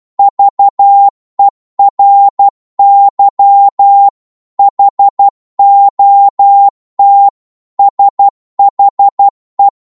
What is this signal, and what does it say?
Morse code, heard as a keyed tone, about 12 wpm, 800 Hz. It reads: VERY HOT SHE